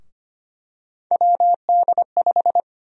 Morse code, heard as a keyed tone, about 25 wpm, 700 Hz. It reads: WD5
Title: USA Call Sign Prefixes - 25wpm